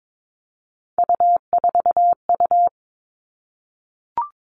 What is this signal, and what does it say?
Morse code, heard as a keyed tone, about 22 wpm, 700 Hz. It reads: U4U
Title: All Cut Numbers - 3-Digit 22wpm